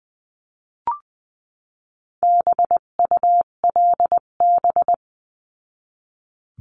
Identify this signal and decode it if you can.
Morse code, heard as a keyed tone, about 20 wpm, 700 Hz. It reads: BULB
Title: 258 Lesson 42c - Words 16-20wpm